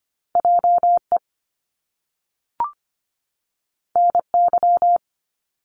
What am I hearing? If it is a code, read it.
Morse code, heard as a keyed tone, about 25 wpm, 700 Hz. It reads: JE NY
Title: Top Two Letter Combinations in Words 25wpm